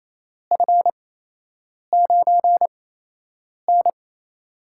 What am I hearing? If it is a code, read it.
Morse code, heard as a keyed tone, about 28 wpm, spaced as for 8 wpm, 700 Hz. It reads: F9N
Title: USA Call Signs - 4 Characters - 8x Character Spacing - 28wpm